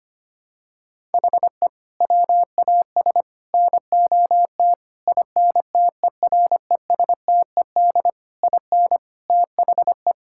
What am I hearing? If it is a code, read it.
Morse code, heard as a keyed tone, about 25 wpm, 700 Hz. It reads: HE WAS NOT INTERESTED IN THE
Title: Sentences from Top 1000 Words - Review Part 6 - 25wpm